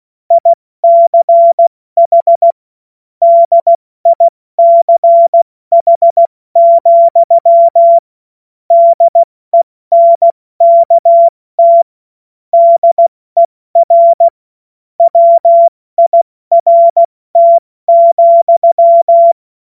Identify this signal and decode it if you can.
Morse code, heard as a keyed tone, about 16 wpm, 680 Hz. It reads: ICH DICH, DENKT DER WIRT,
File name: Morsekurs / Witze / Witz 11. Mai 2021, 70 F80